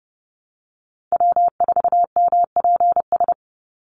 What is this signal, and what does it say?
Morse code, heard as a keyed tone, about 30 wpm, 700 Hz. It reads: W4MPS
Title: NA SOTA Chaser Callsigns 30wpm